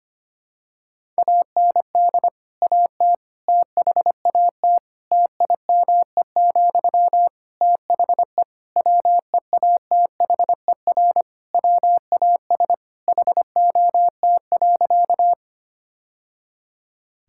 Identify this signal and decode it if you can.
Morse code, heard as a keyed tone, about 25 wpm, 700 Hz. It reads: AND AT THAT TIME, THE WEATHER WAS HOT.
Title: Sentences from Top 1000 Words - Review Part 5 - 25wpm